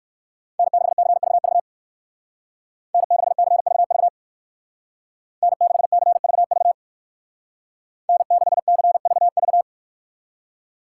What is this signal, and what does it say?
Morse code, heard as a keyed tone, about 56 wpm, 700 Hz. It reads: N6CVV N6CVV N6CVV N6CVV
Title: Speed Racing - USA Call Signs 45wpm